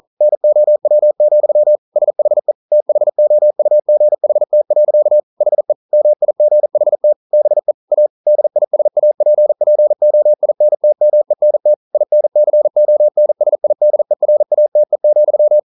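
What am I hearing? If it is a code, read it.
Morse code, heard as a keyed tone, about 42 wpm, 600 Hz. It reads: NOW, SHE THOUGHT. HE MIGHT BE A DISAPPOINTMENT INCONSIDERATE,
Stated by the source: Morse code practice at 41 wpm, from CW med Gnister.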